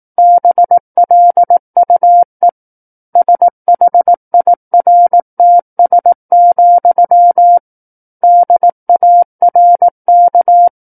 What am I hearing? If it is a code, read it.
Morse code, heard as a keyed tone, about 18 wpm, 700 Hz. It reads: BLUE SHIRTS, DARK
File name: 18 WPM